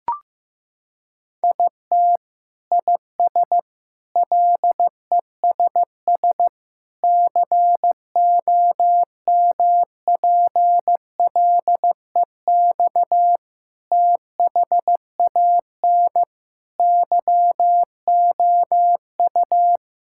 Morse code, heard as a keyed tone, about 16 wpm, 700 Hz. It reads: IT IS LESS COMPLEX THAN YOU
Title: Sentences from Top 1000 Words - Review Part 6 - 15wpm